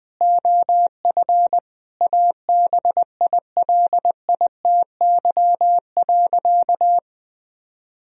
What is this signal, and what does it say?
Morse code, heard as a keyed tone, about 21 wpm, 700 Hz. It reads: OF ABILITY.
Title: Sentences from Top 1000 Words - Review Part 3 - 20wpm